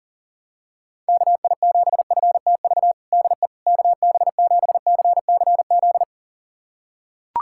Morse code, heard as a keyed tone, about 40 wpm, 700 Hz. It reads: KI7FTV DE KB7CCZ
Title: USA Callsign DE Callsign - 6 Characters - 40wpm